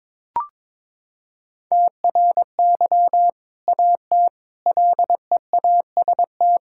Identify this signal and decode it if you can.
Morse code, heard as a keyed tone, about 21 wpm, 700 Hz. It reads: TRY AT LEAST
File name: practice / 22wpm / Sentences with 2-Word Phrases - 22wpm